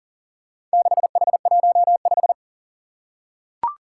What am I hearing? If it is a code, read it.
Morse code, heard as a keyed tone, about 40 wpm, 700 Hz. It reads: 6H15
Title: Four Challenging Letter-Number Combinations 40wpm